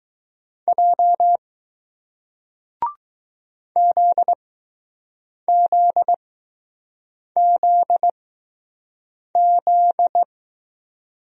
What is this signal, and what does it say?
Morse code, heard as a keyed tone, about 20 wpm, spaced as for 6 wpm, 700 Hz. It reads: J ZZZZ